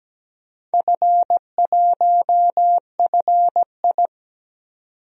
Morse code, heard as a keyed tone, about 17 wpm, 700 Hz. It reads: F1FI
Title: DXCC-100 Call Signs - 4 Characters - 17wpm